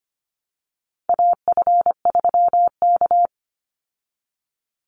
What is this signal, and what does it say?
Morse code, heard as a keyed tone, about 25 wpm, 700 Hz. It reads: AF3K
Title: Call Signs Repeated 3x - 25wpm